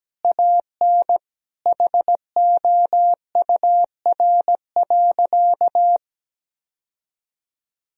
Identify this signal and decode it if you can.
Morse code, heard as a keyed tone, about 17 wpm, 700 Hz. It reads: AN HOUR.
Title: Sentences from Top 1000 Words - Review Part 1 - 17wpm